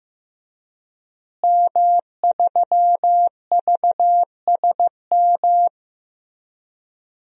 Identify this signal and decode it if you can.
Morse code, heard as a keyed tone, about 14 wpm, 700 Hz. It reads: M3VSM
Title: DXCC-100 Call Signs - 5 Characters - Repeated 3x - 15wpm